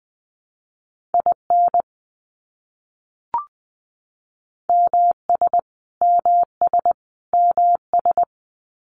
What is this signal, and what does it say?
Morse code, heard as a keyed tone, about 20 wpm, 700 Hz. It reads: IN MS MS MS
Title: US States Repeated 3x 20wpm